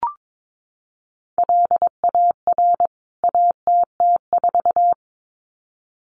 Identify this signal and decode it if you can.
Morse code, heard as a keyed tone, about 22 wpm, 700 Hz. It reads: LAR ATT4
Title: CWT Exchange 22wpm